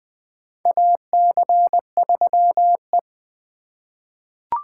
Morse code, heard as a keyed tone, about 20 wpm, 700 Hz. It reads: AC3E
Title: USA Call Signs - 4 Characters - 2x Character Spacing - 20wpm